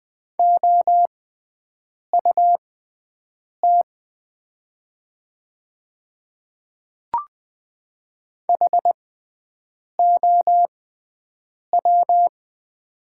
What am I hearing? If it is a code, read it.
Morse code, heard as a keyed tone, about 20 wpm, spaced as for 7 wpm, 700 Hz. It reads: OUT HOW
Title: 6x Character Spacing - Top 100 Words - 20wpm